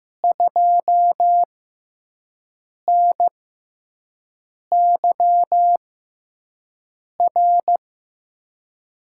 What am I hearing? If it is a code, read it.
Morse code, heard as a keyed tone, about 15 wpm, spaced as for 5 wpm, 700 Hz. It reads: 2NYR
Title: USA Call Signs - 6 Characters - 6x Character Spacing - 15wpm